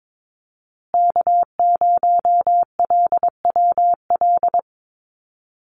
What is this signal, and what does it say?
Morse code, heard as a keyed tone, about 22 wpm, 700 Hz. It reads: K0LWL
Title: USA Call Signs Repeated 3x 22wpm